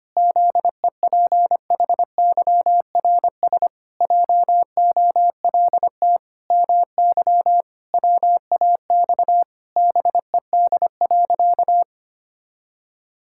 Morse code, heard as a keyed tone, about 25 wpm, 700 Hz. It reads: ZEPHYRS JOLT MY WAX BED.
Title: Pangram Challenge v2 - 25wpm